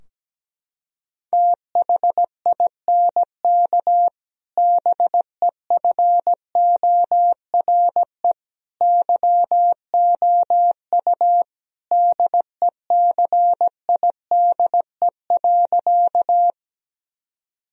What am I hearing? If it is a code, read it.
Morse code, heard as a keyed tone, about 17 wpm, 700 Hz. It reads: THINK BEFORE YOU DECIDE.
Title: Sentences from Top 1000 Words - Review Part 3 - 17wpm